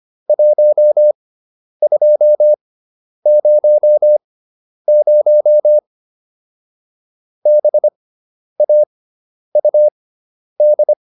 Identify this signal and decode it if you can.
Morse code, heard as a keyed tone, about 25 wpm, spaced as for 10 wpm, 600 Hz. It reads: 1200 BAUD